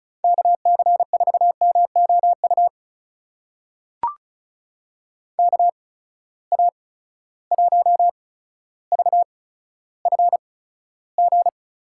Morse code, heard as a keyed tone, about 34 wpm, spaced as for 15 wpm, 700 Hz. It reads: KC4MOU KA1VFG
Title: USA Call Signs - 6 Characters - 8x Character Spacing - 35wpm